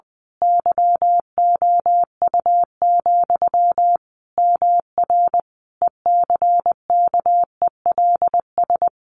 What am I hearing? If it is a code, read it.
Morse code, heard as a keyed tone, about 20 wpm, 700 Hz. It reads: YOU, MR ECKELS
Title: A Sound Of Thunder 20wpm